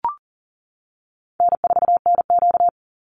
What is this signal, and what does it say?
Morse code, heard as a keyed tone, about 40 wpm, 700 Hz. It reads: N4NQ
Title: Easy Call Signs - 40wpm